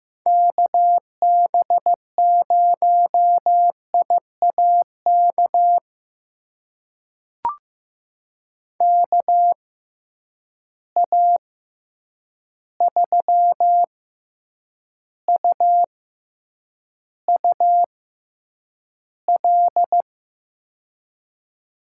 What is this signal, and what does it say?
Morse code, heard as a keyed tone, about 15 wpm, spaced as for 8 wpm, 700 Hz. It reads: KB0IAK KA3UUL
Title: USA Call Signs - 6x Character Spacing - 15wpm